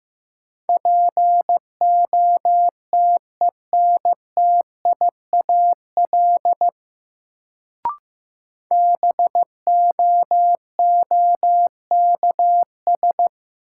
Morse code, heard as a keyed tone, about 15 wpm, 700 Hz. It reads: POTENTIAL BOOKS
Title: Sentences from Top 1000 Words - Review Part 4 - New Word Drill - 15wpm